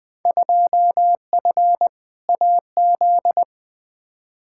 Morse code, heard as a keyed tone, about 20 wpm, 700 Hz. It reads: 2F AZ
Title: Field Day Exchanges - 20wpm